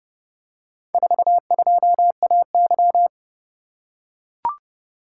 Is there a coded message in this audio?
Yes: Morse code from 0.9 to 3.1 s, heard as a keyed tone, about 30 wpm, 700 Hz. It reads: 42AY